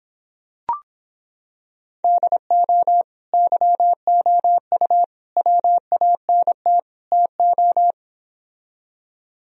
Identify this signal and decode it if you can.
Morse code, heard as a keyed tone, about 26 wpm, 700 Hz. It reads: DO YOU WANT TO